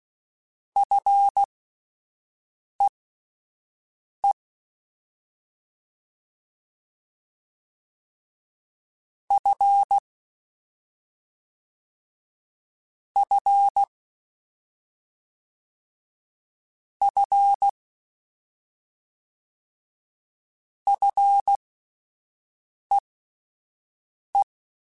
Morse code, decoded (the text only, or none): FEE F F F FEE